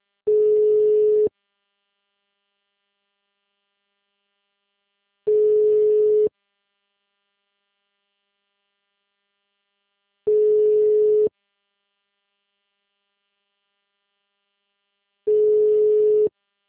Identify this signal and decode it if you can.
Ringback tone (Europe, 425 Hz).